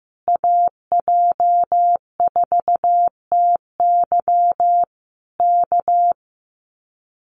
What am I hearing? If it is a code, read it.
Morse code, heard as a keyed tone, about 14 wpm, 700 Hz. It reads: AJ4TY K
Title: Common QSO Phrases Encore 15wpm